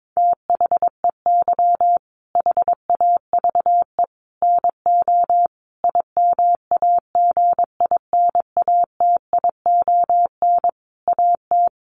Morse code, heard as a keyed tone, about 22 wpm, 700 Hz. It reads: THEY HAVE NO IMAGINATION AT
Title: Sentences from Top 1300 Words 22wpm